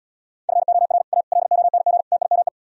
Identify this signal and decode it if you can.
Morse code, heard as a keyed tone, about 78 wpm, 690 Hz. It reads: FOR A LONG TIME